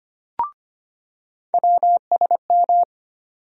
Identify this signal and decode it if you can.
Morse code, heard as a keyed tone, about 25 wpm, 700 Hz. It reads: WSM
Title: USA Call Sign Postfixes - 3 Letters 25wpm